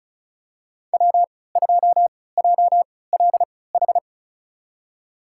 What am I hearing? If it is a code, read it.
Morse code, heard as a keyed tone, about 36 wpm, spaced as for 20 wpm, 700 Hz. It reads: W2JLH
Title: USA Call Signs - 5 Characters - 3x Character Spacing - 35wpm